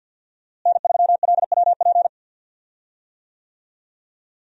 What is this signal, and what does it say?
Morse code, heard as a keyed tone, about 48 wpm, 700 Hz. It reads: N3LWP
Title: USA Call Signs - 5 Characters - Repeated 2x - 50wpm